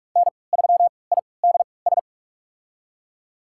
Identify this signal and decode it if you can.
Morse code, heard as a keyed tone, about 42 wpm, spaced as for 25 wpm, 700 Hz. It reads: N3IDS
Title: USA Call Signs - 3x Character Spacing - 45wpm